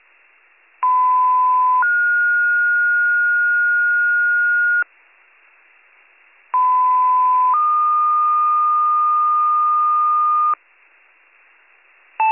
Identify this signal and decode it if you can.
Two-tone page, heard as a one-second tone, then a three-second tone, about 1010 then 1430 Hz, about 1010 then 1220 Hz. It is heard from 0.8 to 10.5 s.